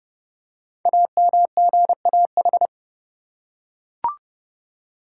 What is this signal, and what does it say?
Morse code, heard as a keyed tone, about 30 wpm, 700 Hz. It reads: AMGAH
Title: Code Groups - 5 Letters - 30wpm